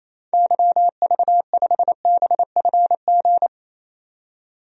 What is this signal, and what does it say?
Morse code, heard as a keyed tone, about 27 wpm, 700 Hz. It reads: YV5BFG
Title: DXCC-100 Call Signs - 6 Characters - 28wpm